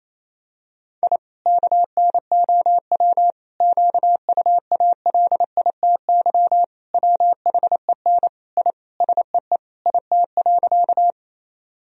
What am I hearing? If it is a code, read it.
Morse code, heard as a keyed tone, about 28 wpm, 700 Hz. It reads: I KNOW QUALITY WHEN I SEE IT.
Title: Sentences from Top 1000 Words - Review Part 4 - 28wpm